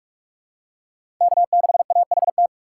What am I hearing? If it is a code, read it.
Morse code, heard as a keyed tone, about 45 wpm, 700 Hz. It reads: K6AHT
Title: USA Call Signs - 5 Characters - 3x Character Spacing - 45wpm